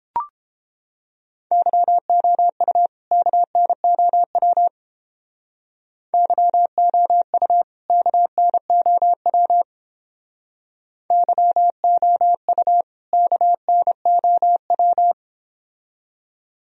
Morse code, heard as a keyed tone, about 30 wpm, 700 Hz. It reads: YOU KNOW YOU KNOW YOU KNOW